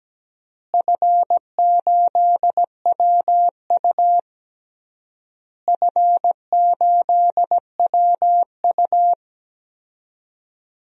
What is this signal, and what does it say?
Morse code, heard as a keyed tone, about 17 wpm, 700 Hz. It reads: F8WU F8WU
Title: DXCC-100 Call Signs - 4 Characters - Repeated 3x - 17wpm